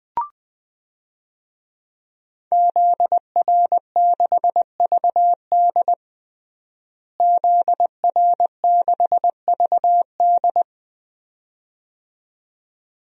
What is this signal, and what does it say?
Morse code, heard as a keyed tone, about 20 wpm, 700 Hz. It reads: ZR6VD ZR6VD
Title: DXCC-100 Call Signs - 5 Characters - Repeated 2x - 20wpm